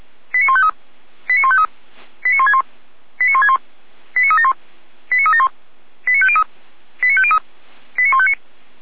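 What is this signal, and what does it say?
Five-tone selective call (ZVEI), heard as quick runs of five tones, tones of about 70 ms. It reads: 89153 89153 89161 89161 89371 89371 89503 89503 89159
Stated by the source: ZVEI five-tone call received on SDR, from audio.f85360.feuerwehr-orlamuende.org.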